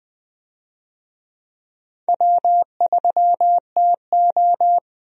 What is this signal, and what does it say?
Morse code, heard as a keyed tone, about 20 wpm, 700 Hz. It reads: W3TO
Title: USA Call Signs - 4 Characters - Repeated 3x - 20wpm